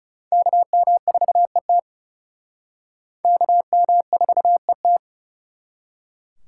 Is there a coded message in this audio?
Yes: Morse code from 0.3 to 5.0 s, heard as a keyed tone, about 38 wpm, 700 Hz. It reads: KM4ET KM4ET